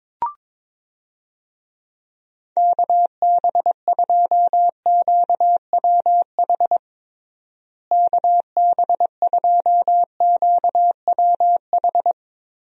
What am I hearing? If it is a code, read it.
Morse code, heard as a keyed tone, about 22 wpm, 700 Hz. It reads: KB2QWH KB2QWH
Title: USA Call Signs - 6 Characters - Repeated 3x - 22wpm